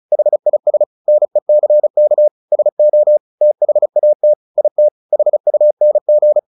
Morse code, heard as a keyed tone, about 34 wpm, 600 Hz. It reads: HIS NECK SO THAT IT HUNG
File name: the old man and the sea / the old man and the sea 1 35wpm